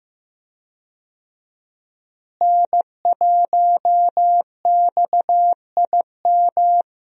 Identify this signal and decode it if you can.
Morse code, heard as a keyed tone, about 15 wpm, 700 Hz. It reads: N1XIM
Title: USA Call Signs - 5 Characters - Repeated 2x - 15wpm